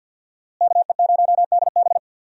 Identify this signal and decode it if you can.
Morse code, heard as a keyed tone, about 46 wpm, 700 Hz. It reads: KE0DB
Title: USA Call Signs - 5 Characters - Repeated 3x - 50wpm